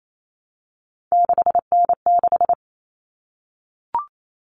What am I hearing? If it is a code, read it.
Morse code, heard as a keyed tone, about 28 wpm, 700 Hz. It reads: 6N6
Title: All Cut Numbers - 3-Digit 28wpm